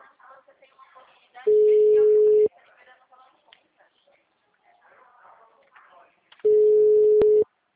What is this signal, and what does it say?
Ringback tone (Europe, 425 Hz), with speech.